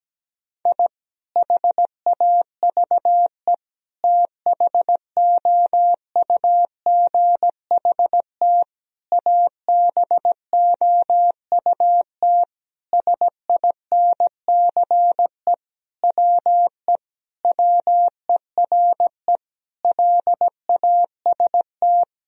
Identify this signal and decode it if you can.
Morse code, heard as a keyed tone, about 17 wpm, 700 Hz. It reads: I HAVE THOUGHT ABOUT SINCE WE WERE LAST
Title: Sentences from Top 1000 Words - Review Part 1 - 17wpm